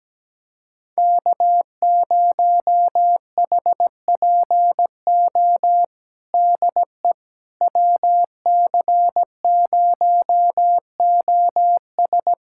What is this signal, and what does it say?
Morse code, heard as a keyed tone, about 17 wpm, 700 Hz. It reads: K0HPO DE WC0OS
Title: USA Callsign DE Callsign - 5 Characters - 17wpm